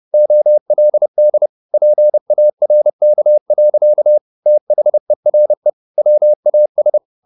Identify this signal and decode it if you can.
Morse code, heard as a keyed tone, about 30 wpm, 600 Hz. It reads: OLD PARK. THERE WAS
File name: the old man and the sea / the old man and the sea 1 30wpm